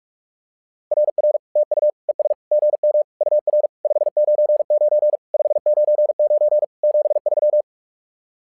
Morse code, heard as a keyed tone, about 46 wpm, 600 Hz. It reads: RR TU ES GM UR 599 599 73
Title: SOTA QSO Practice 45wpm